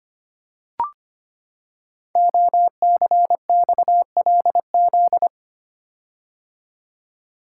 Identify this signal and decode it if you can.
Morse code, heard as a keyed tone, about 25 wpm, 700 Hz. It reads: OCXLZ
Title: Code Groups - 5 Letters - 25wpm